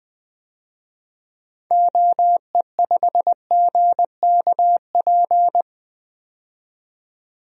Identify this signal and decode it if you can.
Morse code, heard as a keyed tone, about 20 wpm, 700 Hz. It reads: OE5GKP
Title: DXCC-100 Call Signs - 6 Characters - 20wpm